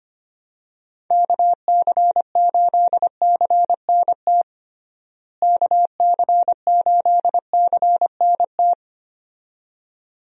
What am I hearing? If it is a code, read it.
Morse code, heard as a keyed tone, about 25 wpm, 700 Hz. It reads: KC8CNT KC8CNT
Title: USA Call Signs - 6 Characters - Repeated 2x - 25wpm